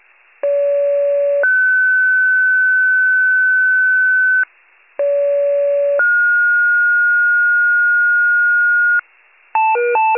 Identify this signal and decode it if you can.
Two-tone page, heard as a one-second tone, then a three-second tone, about 570 then 1500 Hz, about 570 then 1430 Hz. It is heard from 0.4 to 9.0 s.
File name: pager / 73 RESCUE / wcalert